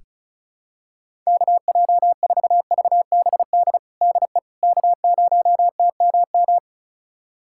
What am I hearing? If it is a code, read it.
Morse code, heard as a keyed tone, about 34 wpm, 700 Hz. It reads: KJ4VBD DE K0TMM
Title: USA Callsign DE Callsign 35wpm